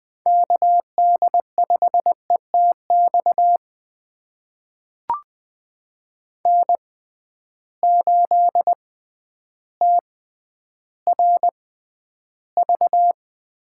Morse code, heard as a keyed tone, about 20 wpm, spaced as for 11 wpm, 700 Hz. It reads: KD5ETX N8TRV
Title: USA Call Signs - 6x Character Spacing - 20wpm